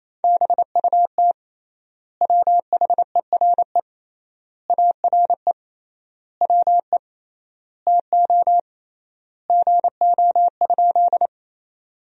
Morse code, heard as a keyed tone, about 28 wpm, 700 Hz. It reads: BUT WHERE ARE WE TO GO?
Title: 3x Word-Spacing - Sentences from Top 200 Words - 28wpm